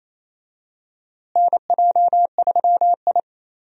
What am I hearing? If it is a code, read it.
Morse code, heard as a keyed tone, about 28 wpm, 700 Hz. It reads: NJ3I
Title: USA Call Signs - 4 Characters - Repeated 2x - 28wpm